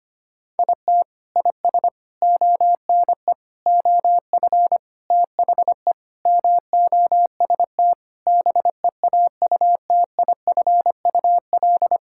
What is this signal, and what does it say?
Morse code, heard as a keyed tone, about 25 wpm, 700 Hz. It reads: IT IS ONE OF THE MOST BEAUTIFUL
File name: practice / 25wpm / Sentences with 3-Word Phrases - 25wpm